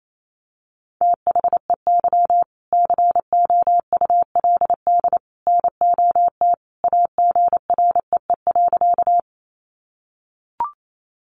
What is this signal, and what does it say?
Morse code, heard as a keyed tone, about 28 wpm, 700 Hz. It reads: THEY COULD NOT AGREE.
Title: Sentences from Top 1100 Words 28wpm